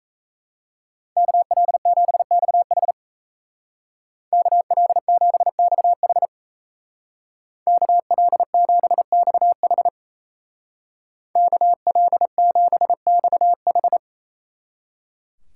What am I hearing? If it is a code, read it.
Morse code, heard as a keyed tone, about 35 wpm, 700 Hz. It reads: KL7XH KL7XH KL7XH KL7XH